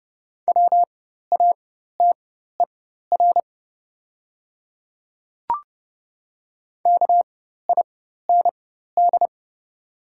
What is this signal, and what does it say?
Morse code, heard as a keyed tone, about 30 wpm, spaced as for 14 wpm, 700 Hz. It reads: WATER KIND
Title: 4x Character Spacing - Top 200 Words - 30wpm